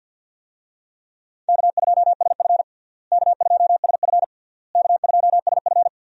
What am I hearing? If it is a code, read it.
Morse code, heard as a keyed tone, about 52 wpm, 700 Hz. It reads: K2SF K2SF K2SF
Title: USA Call Signs - 4 Characters - Repeated 3x - 50wpm